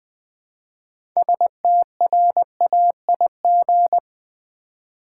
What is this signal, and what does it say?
Morse code, heard as a keyed tone, about 20 wpm, 700 Hz. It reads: STRAIG
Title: Build A Word - 2 Letters At A Time - 8-Letter Words - 20wpm